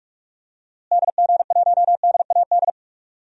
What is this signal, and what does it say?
Morse code, heard as a keyed tone, about 46 wpm, 700 Hz. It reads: DG1DAD
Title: DXCC-100 Call Signs - 6 Characters - Repeated 2x - 45wpm